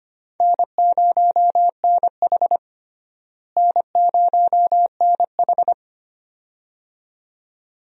Morse code, heard as a keyed tone, about 25 wpm, 700 Hz. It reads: N0NH N0NH
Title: USA Call Signs - 4 Characters - Repeated 2x - 25wpm